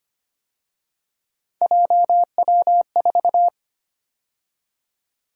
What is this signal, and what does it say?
Morse code, heard as a keyed tone, about 31 wpm, 700 Hz. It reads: JW4
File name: Speed-Racing - Three Letter-Number Combinations - 25wpm